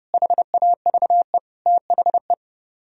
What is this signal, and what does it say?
Morse code, heard as a keyed tone, about 38 wpm, 700 Hz. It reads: HAVE THE